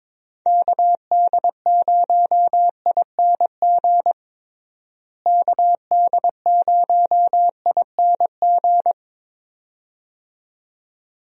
Morse code, heard as a keyed tone, about 22 wpm, 700 Hz. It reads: KD0ING KD0ING
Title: USA Call Signs - 6 Characters - Repeated 2x - 22wpm